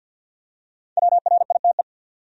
Morse code, heard as a keyed tone, about 53 wpm, 700 Hz. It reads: WRITE
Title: Top 100 Words 50wpm